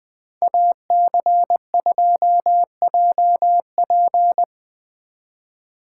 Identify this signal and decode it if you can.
Morse code, heard as a keyed tone, about 24 wpm, 700 Hz. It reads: AC2JP
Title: Speed Racing - USA 2x2 Call Signs 20wpm